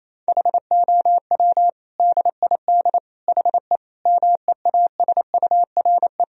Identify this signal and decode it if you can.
Morse code, heard as a keyed tone, about 28 wpm, 700 Hz. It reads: HOW DID HE MEASURE
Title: Sentences from Top 1000 Words - Review Part 2 - 28wpm